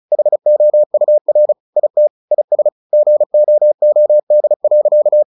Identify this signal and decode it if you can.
Morse code, heard as a keyed tone, about 34 wpm, 600 Hz. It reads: HOUR IT IS GOOD.
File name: the old man and the sea 2 35wpm